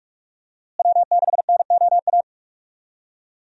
Morse code, heard as a keyed tone, about 46 wpm, 700 Hz. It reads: W6NOA
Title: USA Call Signs - 5 Characters - Repeated 2x - 45wpm